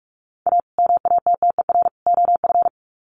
Morse code, heard as a keyed tone, about 44 wpm, 700 Hz. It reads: A MATTER OF
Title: Sets of 3 Words 45wpm